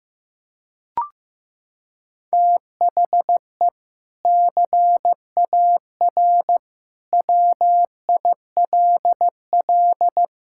Morse code, heard as a keyed tone, about 16 wpm, 700 Hz. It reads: THE CAR WILL
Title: Sentences from Top 1000 Words - Review Part 6 - 15wpm